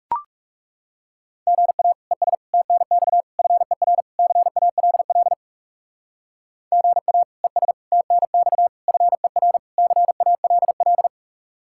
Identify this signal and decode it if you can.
Morse code, heard as a keyed tone, about 36 wpm, 700 Hz. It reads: GA ES TNX FER CALL GA ES TNX FER CALL
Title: Speed-Racing Common QSO Phrases - 30wpm